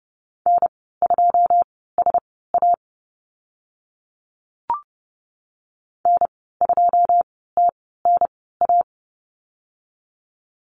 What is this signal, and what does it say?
Morse code, heard as a keyed tone, about 30 wpm, spaced as for 17 wpm, 700 Hz. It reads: N2SA N2TNA
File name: Lesson 14g - Number 2 - Call Signs 30wpm